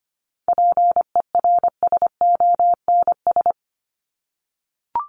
Morse code, heard as a keyed tone, about 25 wpm, 700 Hz. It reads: PERSONS
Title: ES:S Suffix - Top 100 Words with Base Words 25wpm